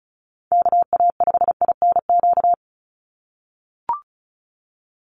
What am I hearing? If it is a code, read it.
Morse code, heard as a keyed tone, about 34 wpm, 700 Hz. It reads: KA5INQ
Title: USA Call Signs 35wpm